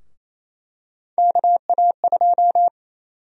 Morse code, heard as a keyed tone, about 28 wpm, 700 Hz. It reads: KA2
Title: USA Call Sign Prefixes - Repeated 2x - 28wpm